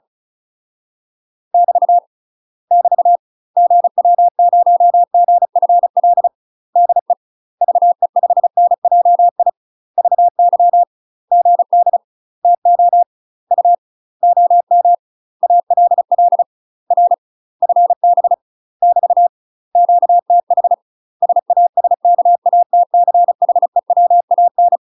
Morse code, heard as a keyed tone, about 34 wpm, 700 Hz. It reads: <BT> <BT> GW0GFL DE VE5NJI VY GD TO U OM ALL R FB <BT> QTH SASKATCHEWAN